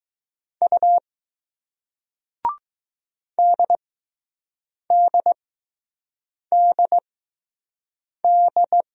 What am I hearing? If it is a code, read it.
Morse code, heard as a keyed tone, about 20 wpm, spaced as for 6 wpm, 700 Hz. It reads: U DDDD